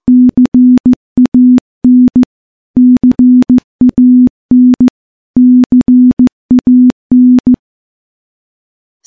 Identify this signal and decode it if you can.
Morse code, heard as a keyed tone, about 15 wpm, 250 Hz. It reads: CAN CAN CAN
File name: morse / morse-training-15WPM